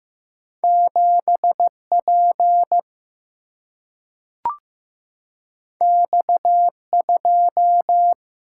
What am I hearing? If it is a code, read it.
Morse code, heard as a keyed tone, about 15 wpm, 700 Hz. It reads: 7P X2
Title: Single-Letter Single-Number 15wpm